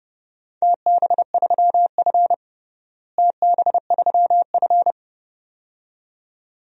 Morse code, heard as a keyed tone, about 30 wpm, 700 Hz. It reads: TB3F TB3F
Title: DXCC-100 Call Signs - 4 Characters - Repeated 2x - 30wpm